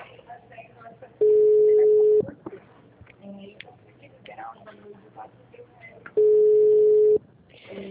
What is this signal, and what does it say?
Ringback tone (Europe, 425 Hz), with speech.
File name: out- -2013- - - .84722